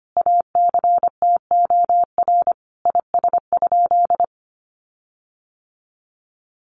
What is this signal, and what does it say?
Morse code, heard as a keyed tone, about 25 wpm, 700 Hz. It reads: ACTOR IS?
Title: Sentences from Top 1500 Words 25wpm